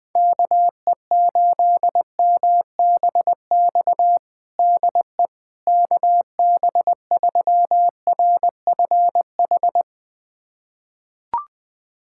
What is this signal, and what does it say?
Morse code, heard as a keyed tone, about 19 wpm, 700 Hz. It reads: KE8MBX DE KB3RFH
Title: USA Callsign DE Callsign 20wpm